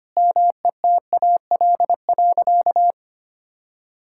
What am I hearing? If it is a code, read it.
Morse code, heard as a keyed tone, about 25 wpm, 700 Hz. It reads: METAL.
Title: Sentences from Top 1600 Words 25wpm